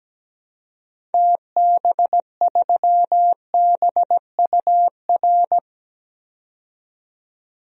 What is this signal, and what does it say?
Morse code, heard as a keyed tone, about 17 wpm, 700 Hz. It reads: TB3BUR
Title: DXCC-100 Call Signs - 6 Characters - 17wpm